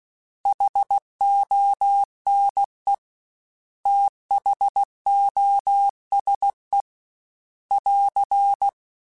Morse code, heard as a keyed tone, about 16 wpm, 800 Hz. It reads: HONE THOSE <AR>
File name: T then H Runs / T then H Runs.mp3 / T then H Runs @15